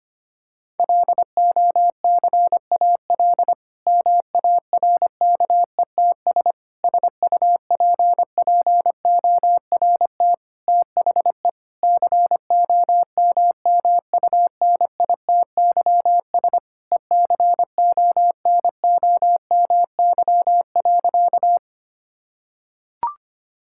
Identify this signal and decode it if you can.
Morse code, heard as a keyed tone, about 25 wpm, 700 Hz. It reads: LOCAL MARKETS SUPPORT THE COMMUNITYS ECONOMY.
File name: practice / 25wpm / Difficult Sentences with Hints - 25wpm